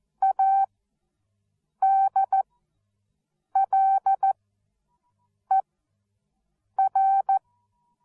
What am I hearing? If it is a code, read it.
Morse code, heard as a keyed tone, about 14 wpm, spaced as for 6 wpm, 780 Hz. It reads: ADLER
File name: Onlinelehrgang / morsen / DJ4UF Morsen13